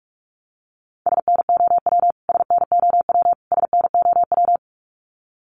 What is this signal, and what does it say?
Morse code, heard as a keyed tone, about 44 wpm, 700 Hz. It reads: SNOW SNOW SNOW
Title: Top 500 Words Repeated 45wpm